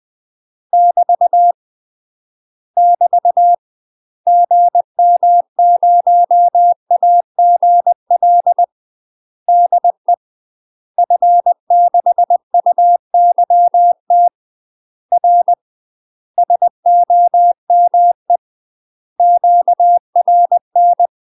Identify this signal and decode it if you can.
Morse code, heard as a keyed tone, about 20 wpm, 700 Hz. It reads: <BT> <BT> GM0AGL DE F6UYT R SOME QRN
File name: UK MASTER QSOs TRAINING / 20WPM / UK MASTER QSOs 0074